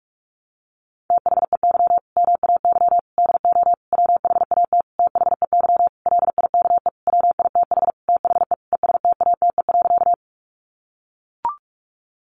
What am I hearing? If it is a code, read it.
Morse code, heard as a keyed tone, about 44 wpm, 700 Hz. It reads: THEY MAY DO WHAT THEY LIKE WITH THE ESTATE.
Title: Sentences from Top 1200 Words 45wpm